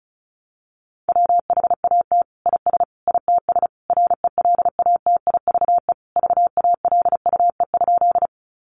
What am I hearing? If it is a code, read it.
Morse code, heard as a keyed tone, about 34 wpm, 700 Hz. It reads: WHAT IS ITS RELATIVE VALUE?
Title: Sentences from Top 1400 Words 35wpm